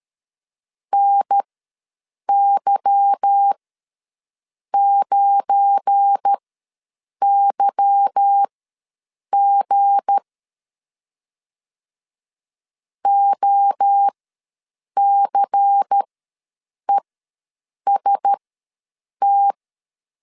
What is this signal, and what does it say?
Morse code, heard as a keyed tone, about 13 wpm, spaced as for 7 wpm, 800 Hz. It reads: NY9YG OCEST
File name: LEZIONI / 8 YJ45 / 8 Y ws 12wpm 3409